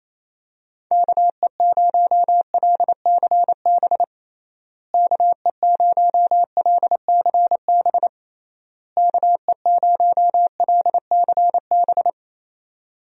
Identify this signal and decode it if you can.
Morse code, heard as a keyed tone, about 28 wpm, 700 Hz. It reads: KE0LCB KE0LCB KE0LCB
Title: USA Call Signs - 6 Characters - Repeated 3x - 28wpm